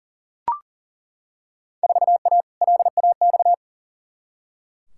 Morse code, heard as a keyed tone, about 40 wpm, 700 Hz. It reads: 4A LAX
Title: Field Day Exchanges - 40wpm